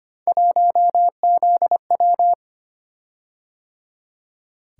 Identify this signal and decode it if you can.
Morse code, heard as a keyed tone, about 25 wpm, 700 Hz. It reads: 1ZW
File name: Three Letter-Number Combinations - 25wpm